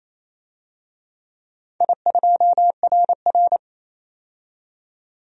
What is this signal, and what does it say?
Morse code, heard as a keyed tone, about 28 wpm, 700 Hz. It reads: I2RR
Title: DXCC-100 Call Signs - 4 Characters - 28wpm